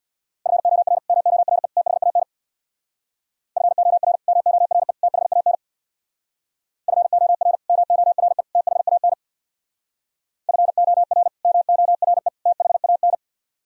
Morse code, heard as a keyed tone, about 64 wpm, 700 Hz. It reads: FOR MORE THAN FOR MORE THAN FOR MORE THAN FOR MORE THAN